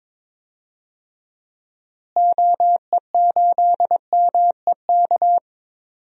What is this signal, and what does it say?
Morse code, heard as a keyed tone, about 21 wpm, 700 Hz. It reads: OE8MEK
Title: DXCC-100 Call Signs - 6 Characters - Repeated 2x - 22wpm